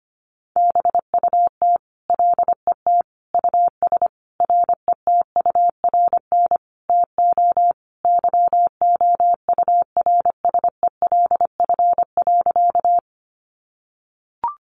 Morse code, heard as a keyed tone, about 25 wpm, 700 Hz. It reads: BUT LET US RETURN TO YOURSELF.
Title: Sentences from Top 800 Words 25wpm